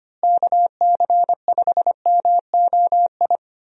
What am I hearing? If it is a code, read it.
Morse code, heard as a keyed tone, about 25 wpm, 700 Hz. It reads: KC5MOI
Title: USA Call Signs - 6 Characters - Repeated 3x - 25wpm